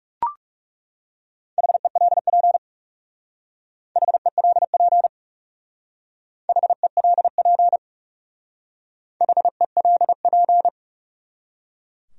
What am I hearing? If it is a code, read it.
Morse code, heard as a keyed tone, about 36 wpm, 700 Hz. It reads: HELP HELP HELP HELP
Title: Speed-Racing Top 200 Words 30wpm